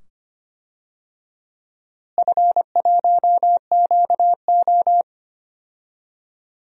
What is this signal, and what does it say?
Morse code, heard as a keyed tone, about 25 wpm, 700 Hz. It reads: F1QO
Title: DXCC-100 Call Signs - 4 Characters - Repeated 2x - 25wpm